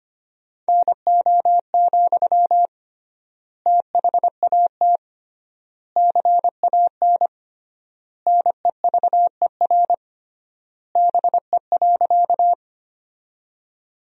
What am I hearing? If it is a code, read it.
Morse code, heard as a keyed tone, about 25 wpm, 700 Hz. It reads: NO, THAT CAN NEVER BE.
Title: 3x Word-Spacing - Sentences from Top 300 Words - 25wpm